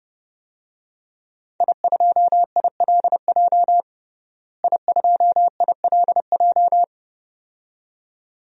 Morse code, heard as a keyed tone, about 30 wpm, 700 Hz. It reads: I2ILJ I2ILJ
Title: DXCC-100 Call Signs - 5 Characters - Repeated 2x - 30wpm